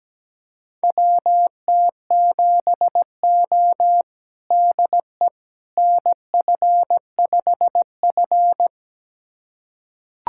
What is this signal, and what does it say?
Morse code, heard as a keyed tone, about 17 wpm, 700 Hz. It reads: WT7O DE NF5F
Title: USA Callsign DE Callsign - 4 Characters - 17wpm